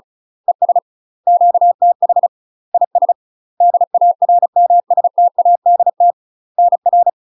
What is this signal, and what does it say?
Morse code, heard as a keyed tone, about 33 wpm, 700 Hz. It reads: ES QTH IS DARMSTADT NR
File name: UK MASTER QSOs TRAINING / 35WPM / UK MASTER QSOs 0010